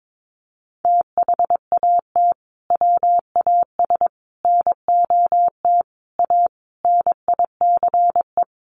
Morse code, heard as a keyed tone, about 22 wpm, 700 Hz. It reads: THAT WAS NOT A NICE
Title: Sentences from Top 1200 Words 22wpm